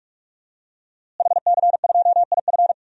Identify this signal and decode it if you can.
Morse code, heard as a keyed tone, about 46 wpm, 700 Hz. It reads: HC2IF
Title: DXCC-100 Call Signs - 5 Characters - Repeated 2x - 45wpm